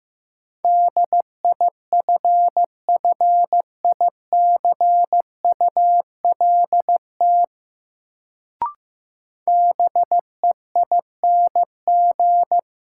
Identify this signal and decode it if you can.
Morse code, heard as a keyed tone, about 15 wpm, 700 Hz. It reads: DIFFICULT BEING
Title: Sentences from Top 600 Words - New Word Drill - 15wpm